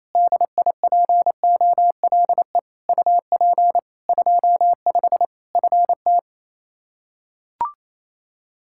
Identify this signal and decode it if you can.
Morse code, heard as a keyed tone, about 28 wpm, 700 Hz. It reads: DIPOLE UP 25 FT
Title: K1USN SST QSO Practice 28wpm